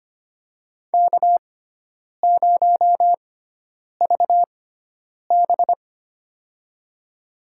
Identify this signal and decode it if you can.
Morse code, heard as a keyed tone, about 25 wpm, spaced as for 9 wpm, 700 Hz. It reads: K0VB
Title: USA Call Signs - 4 Characters - 6x Character Spacing - 25wpm